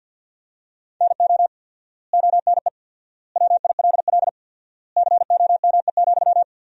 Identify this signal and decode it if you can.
Morse code, heard as a keyed tone, about 53 wpm, 700 Hz. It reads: NO ONE WILL COME,
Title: 4x Word-Spacing - Sentences from Top 100 Words - 50wpm